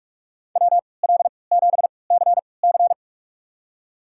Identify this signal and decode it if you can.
Morse code, heard as a keyed tone, about 43 wpm, spaced as for 25 wpm, 700 Hz. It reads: WL7CC
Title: USA Call Signs - 5 Characters - 3x Character Spacing - 45wpm